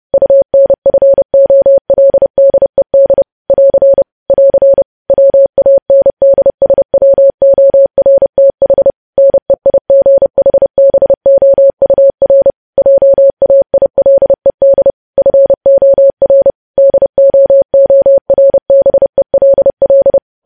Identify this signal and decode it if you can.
Morse code, heard as a keyed tone, about 30 wpm, 550 Hz. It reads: UNFOLDED <AR> <AR> WANDSWORTH NEIGHBOUR JAILED FOR DOORBELL
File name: podcast / News headlines-1x-30x30